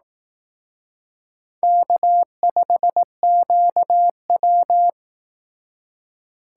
Morse code, heard as a keyed tone, about 18 wpm, 700 Hz. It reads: K5QW
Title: Speed Racing - USA 1x2 Call Signs 15wpm